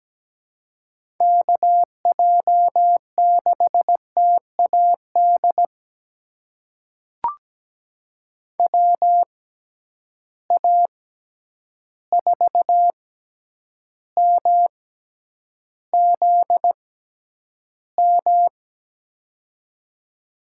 Morse code, heard as a keyed tone, about 17 wpm, spaced as for 9 wpm, 700 Hz. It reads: KJ6TAD WA4MZM
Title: USA Call Signs - 6 Characters - 6x Character Spacing - 17wpm